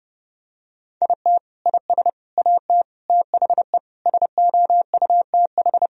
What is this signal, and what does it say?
Morse code, heard as a keyed tone, about 30 wpm, 700 Hz. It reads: IT IS AT THE SOUTH
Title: Sentences from Top 1000 Words - Review Part 2 - 30wpm